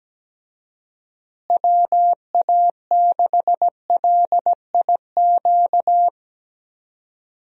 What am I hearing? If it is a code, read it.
Morse code, heard as a keyed tone, about 17 wpm, 700 Hz. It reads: WA6LIQ
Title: USA Call Signs - 6 Characters - Repeated 2x - 17wpm